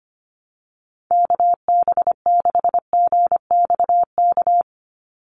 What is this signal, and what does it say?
Morse code, heard as a keyed tone, about 25 wpm, 700 Hz. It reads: KB6GXK
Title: USA Call Signs 25wpm